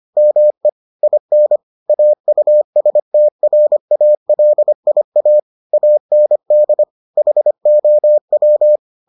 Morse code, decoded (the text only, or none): ME IN AUSTRALIA AND HOW